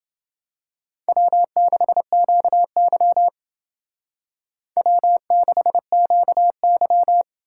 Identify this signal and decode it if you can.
Morse code, heard as a keyed tone, about 28 wpm, 700 Hz. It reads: W6QY W6QY